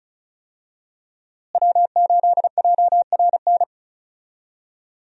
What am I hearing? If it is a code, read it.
Morse code, heard as a keyed tone, about 36 wpm, 700 Hz. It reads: W8JRN
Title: USA Call Signs - 5 Characters - Repeated 3x - 35wpm